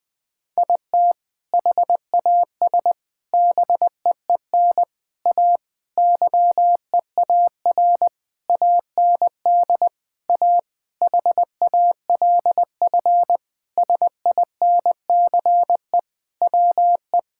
Morse code, heard as a keyed tone, about 21 wpm, 700 Hz. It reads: IT HAS BEEN A YEAR AND A HALF SINCE WE
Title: Sentences with 4-Word Phrases - 20wpm